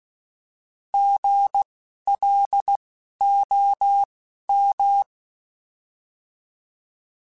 Morse code, heard as a keyed tone, about 16 wpm, spaced as for 11 wpm, 800 Hz. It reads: GLOM
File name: Alphabetical / Alphabetical mp3s / Gs @10